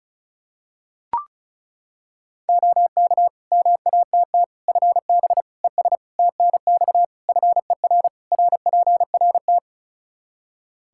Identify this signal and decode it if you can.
Morse code, heard as a keyed tone, about 36 wpm, 700 Hz. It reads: OK MATT FB ES TNX FER RPRT
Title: Practice QSO - Exchange 2 - Info - 35wpm